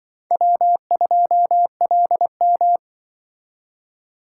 Morse code, heard as a keyed tone, about 25 wpm, 700 Hz. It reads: W2LM